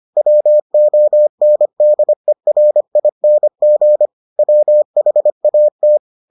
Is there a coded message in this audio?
Yes: Morse code from 0.2 to 6.0 s, heard as a keyed tone, about 25 wpm, 600 Hz. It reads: WONDERING WHAT